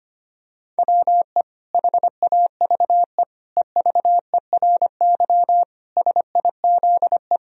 Morse code, heard as a keyed tone, about 25 wpm, 700 Hz. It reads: WE HAVE EVERY SIZE
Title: Sentences from Top 1000 Words - Review Part 2 - 25wpm